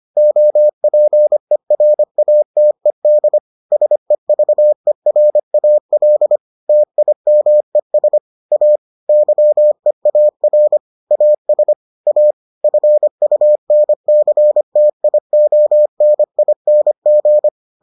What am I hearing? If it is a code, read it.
Morse code, heard as a keyed tone, about 25 wpm, 600 Hz. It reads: OPERATED SEVERAL TIMES A YEAR AS A FUNCTIONING